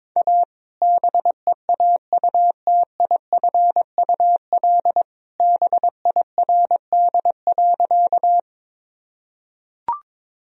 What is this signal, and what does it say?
Morse code, heard as a keyed tone, about 22 wpm, 700 Hz. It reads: A BEAUTIFUL BIRD.
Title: Sentences from Top 1000 Words - Review Part 2 - 22wpm